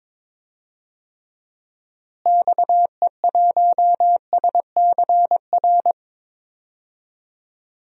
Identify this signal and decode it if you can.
Morse code, heard as a keyed tone, about 22 wpm, 700 Hz. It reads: XE1SCR
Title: DXCC-100 Call Signs - 6 Characters - Repeated 3x - 22wpm